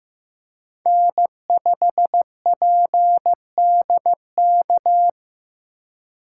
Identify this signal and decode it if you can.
Morse code, heard as a keyed tone, about 15 wpm, 700 Hz. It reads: N5PDK
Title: USA Call Signs - 5 Characters - 6x Character Spacing - 15wpm